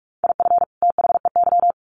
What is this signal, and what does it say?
Morse code, heard as a keyed tone, about 44 wpm, 700 Hz. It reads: IF THEY
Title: Sets of 2 Words 45wpm